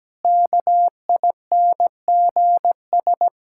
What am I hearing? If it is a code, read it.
Morse code, heard as a keyed tone, about 17 wpm, 700 Hz. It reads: KINGS
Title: Pangram Challenge v2 - 17wpm